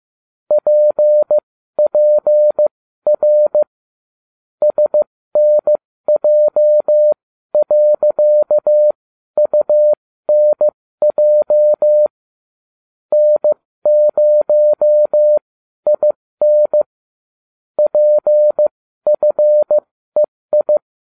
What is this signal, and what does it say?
Morse code, heard as a keyed tone, about 15 wpm, spaced as for 12 wpm, 600 Hz. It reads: PPR SNJ.UNJ N0IN PFEI